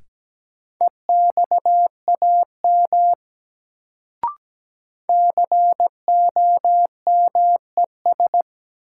Morse code, heard as a keyed tone, about 17 wpm, 700 Hz. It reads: EXAM COMES
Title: Sentences from Top 1000 Words - Review Part 5 - New Word Drill - 17wpm